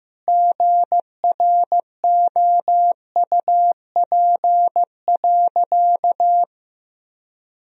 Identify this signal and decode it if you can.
Morse code, heard as a keyed tone, about 15 wpm, 700 Hz. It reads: GROUP.
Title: Sentences from Top 1000 Words - Review Part 1 - 15wpm